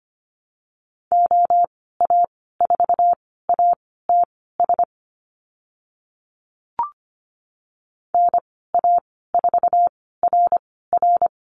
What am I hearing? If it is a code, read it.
Morse code, heard as a keyed tone, about 25 wpm, spaced as for 16 wpm, 690 Hz. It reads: OA4ATS NA4RR
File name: Lesson 15g - Number 5 - Call Signs 25wpm